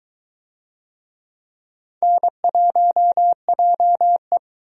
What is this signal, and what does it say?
Morse code, heard as a keyed tone, about 22 wpm, 700 Hz. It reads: N1JE